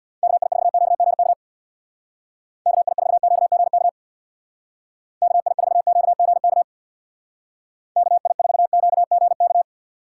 Morse code, heard as a keyed tone, about 65 wpm, 700 Hz. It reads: KI4QGK KI4QGK KI4QGK KI4QGK